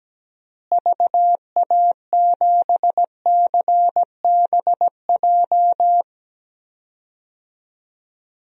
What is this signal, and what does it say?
Morse code, heard as a keyed tone, about 17 wpm, 700 Hz. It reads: VA7CBJ
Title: DXCC-100 Call Signs - 6 Characters - Repeated 3x - 17wpm